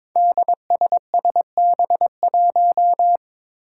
Morse code, heard as a keyed tone, about 21 wpm, 700 Hz. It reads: DSSB1
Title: Five Challenging Letter-Number Combinations 22wpm